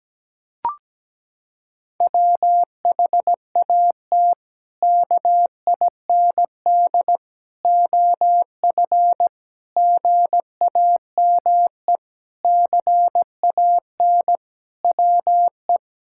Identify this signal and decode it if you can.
Morse code, heard as a keyed tone, about 17 wpm, 700 Hz. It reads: WHAT KIND OF GAME CAN WE
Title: Sentences from Top 1000 Words - Review Part 6 - 17wpm